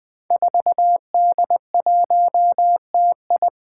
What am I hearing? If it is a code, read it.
Morse code, heard as a keyed tone, about 20 wpm, 700 Hz. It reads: 4D1TI
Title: Five Challenging Letter-Number Combinations 20wpm